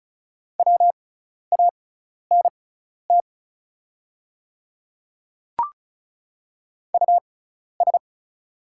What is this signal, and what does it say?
Morse code, heard as a keyed tone, about 36 wpm, spaced as for 12 wpm, 690 Hz. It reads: WANT US
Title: 6x Character Spacing - Top 200 Words - 35wpm